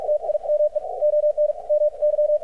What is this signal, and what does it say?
Morse code, heard as a keyed tone, about 48 wpm, spaced as for 41 wpm, 600 Hz. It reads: TIME ON MY
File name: archived / CRSnet